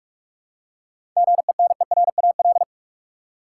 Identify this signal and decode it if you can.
Morse code, heard as a keyed tone, about 48 wpm, 700 Hz. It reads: GENERAL
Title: Seven Letter Words - 45wpm